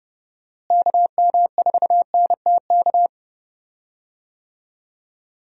Morse code, heard as a keyed tone, about 30 wpm, 700 Hz. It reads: KM4NTK
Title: USA Call Signs - 6 Characters - Repeated 2x - 30wpm